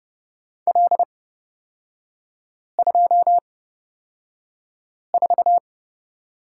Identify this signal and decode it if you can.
Morse code, heard as a keyed tone, about 31 wpm, spaced as for 5 wpm, 700 Hz. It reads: L24
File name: Lesson 38e - Letter q - Three Characters 30wpm